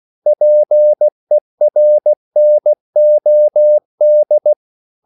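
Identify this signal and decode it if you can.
Morse code, heard as a keyed tone, about 16 wpm, 600 Hz. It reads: PERNOD
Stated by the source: Slow (QRS) Morse code at 16 wpm, from CW med Gnister.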